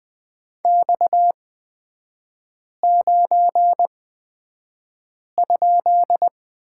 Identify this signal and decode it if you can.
Morse code, heard as a keyed tone, about 20 wpm, spaced as for 5 wpm, 700 Hz. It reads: X9?
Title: 250 Lesson 40e - BK - Three Characters 20wpm